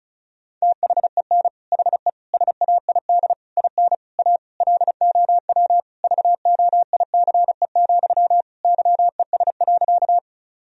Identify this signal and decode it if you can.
Morse code, heard as a keyed tone, about 34 wpm, 700 Hz. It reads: THEN HE SAID IN A LOW VOICE, YES.
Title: Sentences from Top 1000 Words - Review Part 1 - 35wpm